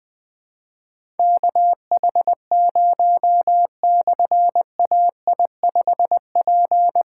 Morse code, heard as a keyed tone, about 20 wpm, 700 Hz. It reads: KH0/AI5P
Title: Difficult Call Signs 20wpm